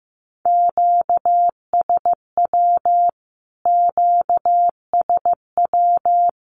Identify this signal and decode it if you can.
Morse code, heard as a keyed tone, about 15 wpm, 700 Hz. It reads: QSW QSW
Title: Q-Codes Repeated 3x 15wpm